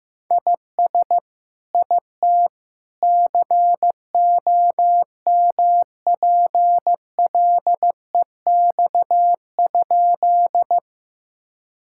Morse code, heard as a keyed tone, about 15 wpm, 700 Hz. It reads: IS IT COMPLEX?
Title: Sentences from Top 1000 Words - Review Part 6 - 15wpm